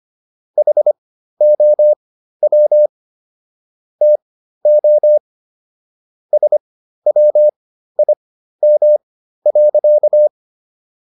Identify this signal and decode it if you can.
Morse code, heard as a keyed tone, about 25 wpm, spaced as for 13 wpm, 600 Hz. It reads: HOW TO SWIM.